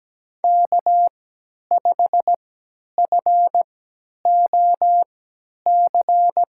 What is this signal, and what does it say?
Morse code, heard as a keyed tone, about 17 wpm, spaced as for 10 wpm, 700 Hz. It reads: K5FOC
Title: USA Call Signs - 5 Characters - 3x Character Spacing - 17wpm